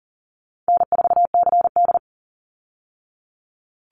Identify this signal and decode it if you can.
Morse code, heard as a keyed tone, about 40 wpm, 700 Hz. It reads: N4CD
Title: Call Signs Repeated 3x - 40wpm